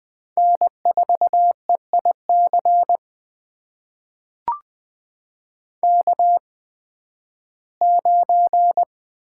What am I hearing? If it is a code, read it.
Morse code, heard as a keyed tone, about 20 wpm, spaced as for 13 wpm, 700 Hz. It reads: N4EIC K9
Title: USA Call Signs - 8x Character Spacing - 20wpm